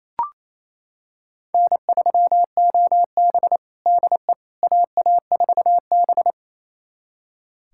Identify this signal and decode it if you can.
Morse code, heard as a keyed tone, about 28 wpm, 700 Hz. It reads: N3OB DE AA4B
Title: USA Callsign DE Callsign - 4 Characters - 28wpm